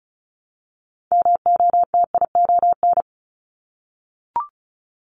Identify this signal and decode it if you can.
Morse code, heard as a keyed tone, about 36 wpm, 700 Hz. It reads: MOTION
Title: Multi-letter Phonemes Repeated 3x 35wpm